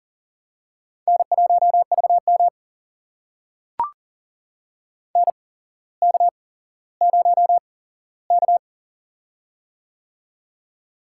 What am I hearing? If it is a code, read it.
Morse code, heard as a keyed tone, about 39 wpm, spaced as for 17 wpm, 700 Hz. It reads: N1VM NK0K
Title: USA Call Signs - 4 Characters - 8x Character Spacing - 40wpm